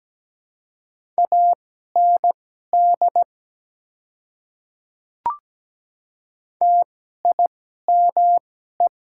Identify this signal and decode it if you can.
Morse code, heard as a keyed tone, about 17 wpm, spaced as for 12 wpm, 700 Hz. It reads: AND TIME
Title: 2x Character Spacing - Top 100 Words - 17wpm